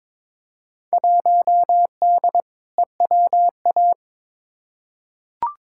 Morse code, heard as a keyed tone, about 22 wpm, 700 Hz. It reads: 1D EWA
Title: Field Day Exchanges - 22wpm